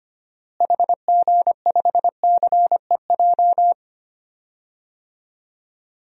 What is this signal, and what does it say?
Morse code, heard as a keyed tone, about 25 wpm, 700 Hz. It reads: HG5CEJ
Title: DXCC-100 Call Signs - 6 Characters - Repeated 3x - 25wpm